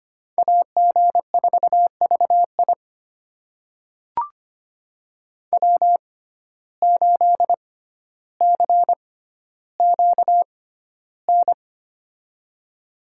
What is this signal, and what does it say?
Morse code, heard as a keyed tone, about 25 wpm, spaced as for 13 wpm, 700 Hz. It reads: AG4VI W8CQN
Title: USA Call Signs - 5 Characters - 6x Character Spacing - 25wpm